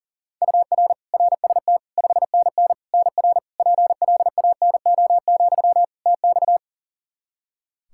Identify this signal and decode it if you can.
Morse code, heard as a keyed tone, about 40 wpm, 700 Hz. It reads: UR RST 5NN NR PLANO, TX
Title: Practice QSO - Exchange 1 - Beginning - 40wpm